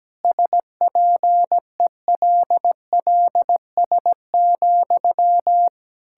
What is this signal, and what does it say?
Morse code, heard as a keyed tone, about 17 wpm, 700 Hz. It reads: SPELLS,
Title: Pangram Challenge v2 - 17wpm